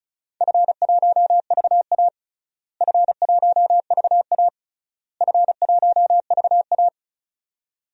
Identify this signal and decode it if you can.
Morse code, heard as a keyed tone, about 36 wpm, 700 Hz. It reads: F1VA F1VA F1VA
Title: DXCC-100 Call Signs - 4 Characters - Repeated 3x - 35wpm